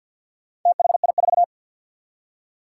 Morse code, heard as a keyed tone, about 51 wpm, 700 Hz. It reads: THI4
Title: Four Challenging Letter-Number Combinations 50wpm